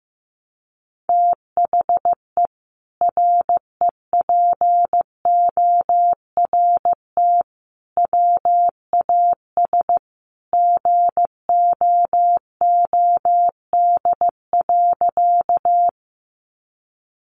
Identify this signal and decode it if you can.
Morse code, heard as a keyed tone, about 15 wpm, 700 Hz. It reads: THE REPORT WAS GOOD.
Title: Sentences from Top 900 Words 15wpm